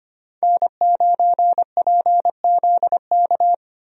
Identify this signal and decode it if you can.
Morse code, heard as a keyed tone, about 25 wpm, 700 Hz. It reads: N9PZK
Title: USA Call Signs - 5 Characters - Repeated 3x - 25wpm